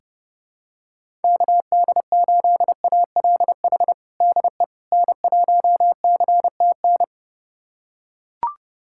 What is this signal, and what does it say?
Morse code, heard as a keyed tone, about 30 wpm, 700 Hz. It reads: KD8ALH DE N1CTN
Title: USA Callsign DE Callsign 30wpm